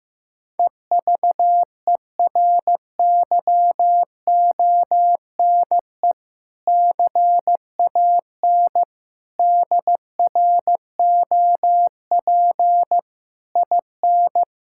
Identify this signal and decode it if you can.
Morse code, heard as a keyed tone, about 15 wpm, 700 Hz. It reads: EVERYONE CAN DROP IN
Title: Sentences from Top 1600 Words 15wpm